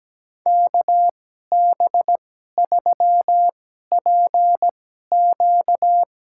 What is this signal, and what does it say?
Morse code, heard as a keyed tone, about 17 wpm, spaced as for 12 wpm, 700 Hz. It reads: KB3PQ
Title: USA Call Signs - 5 Characters - 2x Character Spacing - 17wpm